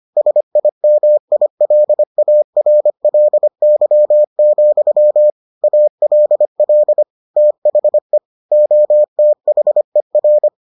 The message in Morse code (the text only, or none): SIMILARLY, ALL THE OTHER